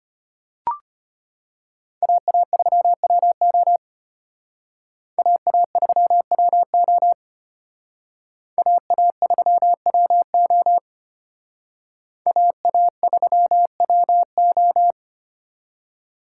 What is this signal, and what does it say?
Morse code, heard as a keyed tone, about 30 wpm, 700 Hz. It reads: AA3WO AA3WO AA3WO AA3WO
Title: Speed Racing - USA 2x2 Call Signs 25wpm